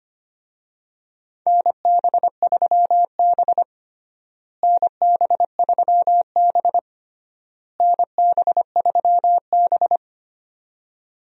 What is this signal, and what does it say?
Morse code, heard as a keyed tone, about 25 wpm, 700 Hz. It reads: NB3B NB3B NB3B
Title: USA Call Signs - 4 Characters - Repeated 3x - 25wpm